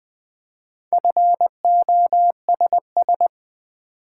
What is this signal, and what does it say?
Morse code, heard as a keyed tone, about 20 wpm, 700 Hz. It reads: FOSS
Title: Build A Word - 2 Letters At A Time - 6-Letter Words - 20wpm